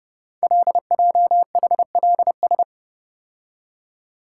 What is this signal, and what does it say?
Morse code, heard as a keyed tone, about 30 wpm, 700 Hz. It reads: LJHLS
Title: Code Groups - 5 Letters - 30wpm